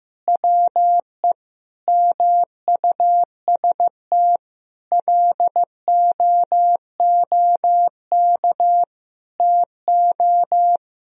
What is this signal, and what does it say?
Morse code, heard as a keyed tone, about 15 wpm, 700 Hz. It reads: WE MUST LOOK TO
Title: Sentences from Top 1000 Words - Review Part 3 - 15wpm